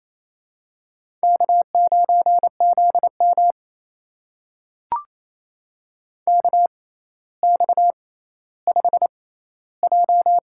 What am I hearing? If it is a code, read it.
Morse code, heard as a keyed tone, about 27 wpm, spaced as for 14 wpm, 700 Hz. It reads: K9ZM KX5J
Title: USA Call Signs - 4 Characters - 6x Character Spacing - 28wpm